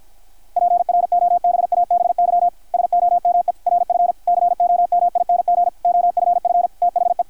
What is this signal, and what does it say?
Morse code, heard as a keyed tone, about 53 wpm, 720 Hz, with radio noise. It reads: PROBABY SOME RF COMING OFF THE
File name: archived / CRSnet